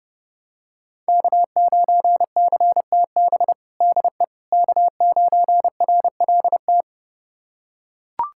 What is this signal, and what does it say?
Morse code, heard as a keyed tone, about 32 wpm, 700 Hz. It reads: K9CTB DE K9RLT
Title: USA Callsign DE Callsign - 5 Characters - 30wpm